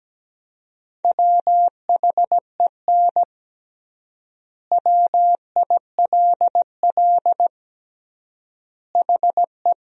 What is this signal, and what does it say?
Morse code, heard as a keyed tone, about 17 wpm, 700 Hz. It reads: WHEN WILL HE
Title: 3x Word-Spacing - Sentences from Top 200 Words - 17wpm